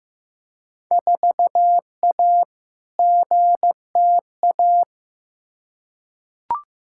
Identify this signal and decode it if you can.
Morse code, heard as a keyed tone, about 15 wpm, 700 Hz. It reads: 4A GTA
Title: Field Day Exchanges Encore - 15wpm